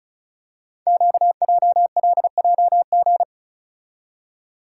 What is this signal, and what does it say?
Morse code, heard as a keyed tone, about 36 wpm, 690 Hz. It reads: QJLJG
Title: Code Groups - 5 Letters - 35wpm